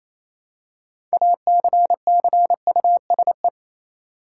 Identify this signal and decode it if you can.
Morse code, heard as a keyed tone, about 27 wpm, 700 Hz. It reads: ACCUSE
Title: Six Letter Words - 28wpm